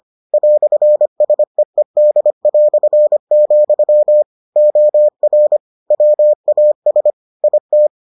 Morse code, heard as a keyed tone, about 25 wpm, 600 Hz. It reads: "SEED", OR WAS IT